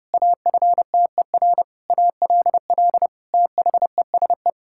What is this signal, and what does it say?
Morse code, heard as a keyed tone, about 32 wpm, 700 Hz. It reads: AFTER ALL THESE
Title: Sentences from Top 1000 Words - Review Part 6 - 30wpm